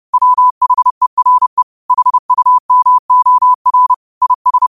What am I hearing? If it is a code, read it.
Morse code, heard as a keyed tone, about 30 wpm, 1000 Hz. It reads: WHERE HUMOR IS
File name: qotdfiles / 30wpm